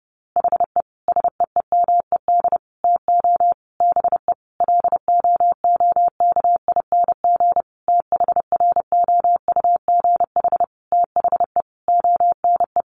Morse code, heard as a keyed tone, about 30 wpm, 700 Hz. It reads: HE SEEMED TO BE LOOKING THROUGH THE ONE
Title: A Sound Of Thunder 30wpm